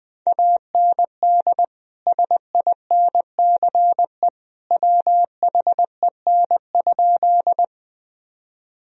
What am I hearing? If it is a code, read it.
Morse code, heard as a keyed tone, about 20 wpm, 700 Hz. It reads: AND SINCE WHEN?
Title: Sentences from Top 1000 Words - Review Part 1 - 20wpm